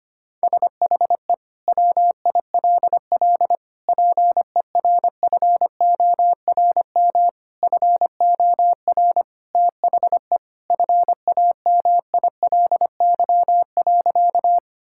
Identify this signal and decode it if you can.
Morse code, heard as a keyed tone, about 25 wpm, 700 Hz. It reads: SHE WILL PERFORM FOR THE FAMILY.
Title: Sentences from Top 1000 Words - Review Part 5 - 25wpm